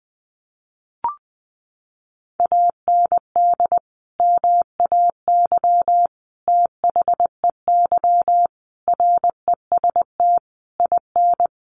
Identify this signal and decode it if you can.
Morse code, heard as a keyed tone, about 20 wpm, 700 Hz. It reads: AND MAY THEY REST IN
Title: Sentences from Top 1500 Words 20wpm